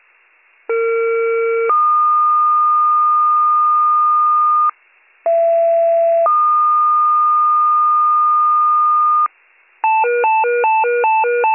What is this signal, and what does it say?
Two-tone page, heard as a one-second tone, then a three-second tone, about 460 then 1220 Hz, about 670 then 1220 Hz. It is heard from 0.7 to 9.3 s.